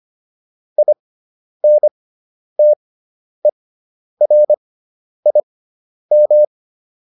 Morse code, heard as a keyed tone, about 25 wpm, spaced as for 10 wpm, 600 Hz. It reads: INTERIM